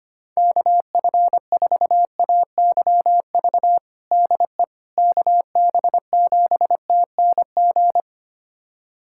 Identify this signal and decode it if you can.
Morse code, heard as a keyed tone, about 25 wpm, 700 Hz. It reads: KF4AYV DE KB7TNG
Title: USA Callsign DE Callsign - 6 Characters - 25wpm